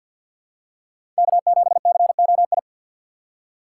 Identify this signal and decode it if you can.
Morse code, heard as a keyed tone, about 47 wpm, 690 Hz. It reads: K7COI
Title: USA Call Signs - 5 Characters - 3x Character Spacing - 50wpm